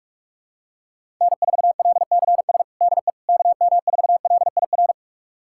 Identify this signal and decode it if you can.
Morse code, heard as a keyed tone, about 46 wpm, 700 Hz. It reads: N4LCS DE KM4LIR
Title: USA Callsign DE Callsign 45wpm